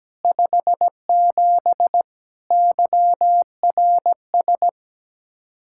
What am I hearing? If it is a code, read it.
Morse code, heard as a keyed tone, about 17 wpm, 700 Hz. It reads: 57 YRS
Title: Speed-Racing Common QSO Phrases - 17wpm